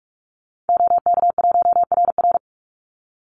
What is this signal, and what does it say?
Morse code, heard as a keyed tone, about 45 wpm, 700 Hz. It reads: OK1RR
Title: Call Signs Repeated 2x - 45wpm